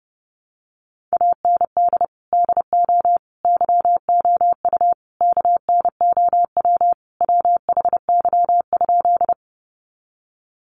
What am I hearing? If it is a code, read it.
Morse code, heard as a keyed tone, about 30 wpm, 700 Hz. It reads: AND DO YOU KNOW WHY?
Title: Sentences from Top 200 Words 30wpm